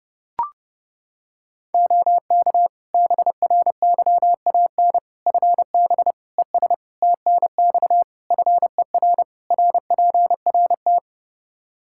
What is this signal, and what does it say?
Morse code, heard as a keyed tone, about 30 wpm, 700 Hz. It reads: OK BRYAN FB ES TNX FER RPRT
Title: Practice QSO - Exchange 2 - Info - 30wpm